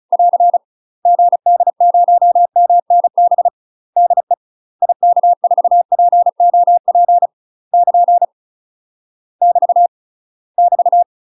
Morse code, heard as a keyed tone, about 34 wpm, 700 Hz. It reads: <AR> GD0MNB DE IK4POP <KN> <BT> <BT>
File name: UK MASTER QSOs TRAINING / 35WPM / UK MASTER QSOs 0015